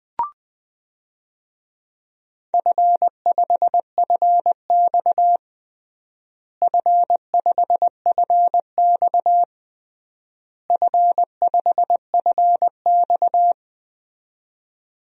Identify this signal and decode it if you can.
Morse code, heard as a keyed tone, about 20 wpm, 700 Hz. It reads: F5FX F5FX F5FX
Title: DXCC-100 Call Signs - 4 Characters - Repeated 3x - 20wpm